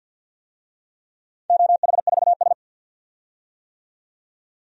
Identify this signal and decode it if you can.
Morse code, heard as a keyed tone, about 46 wpm, 700 Hz. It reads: OH4S
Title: DXCC-100 Call Signs - 4 Characters - 50wpm